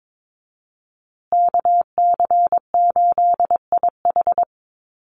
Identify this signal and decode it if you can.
Morse code, heard as a keyed tone, about 22 wpm, 700 Hz. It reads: KC8IH
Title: USA Call Signs Repeated 2x 22wpm